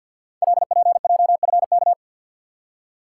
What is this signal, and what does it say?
Morse code, heard as a keyed tone, about 48 wpm, 700 Hz. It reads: LPJFK
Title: Code Groups - 5 Letters - 50wpm